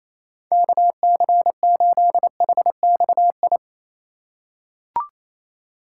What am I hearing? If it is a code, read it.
Morse code, heard as a keyed tone, about 28 wpm, 700 Hz. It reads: KC8HXI
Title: USA Call Signs - 4x Character Spacing - 28wpm